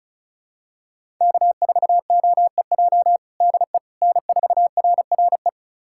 Morse code, heard as a keyed tone, about 36 wpm, 700 Hz. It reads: K4OEJ DE N4RRE
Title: USA Callsign DE Callsign - 5 Characters - 35wpm